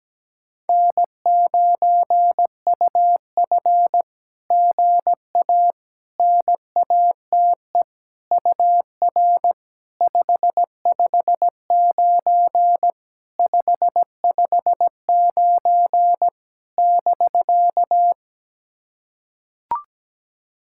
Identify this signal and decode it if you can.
Morse code, heard as a keyed tone, about 18 wpm, 700 Hz. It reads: N9UF GA NATE UR 559 559 <BK>
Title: SOTA QSO Practice 17wpm